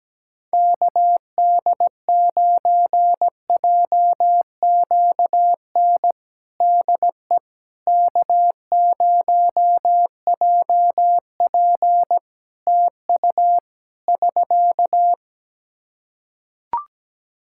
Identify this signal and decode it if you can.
Morse code, heard as a keyed tone, about 17 wpm, 700 Hz. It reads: KD9JQN DE K0JP TU <SK>
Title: Practice QSO - Exchange 4 - Ending - 17wpm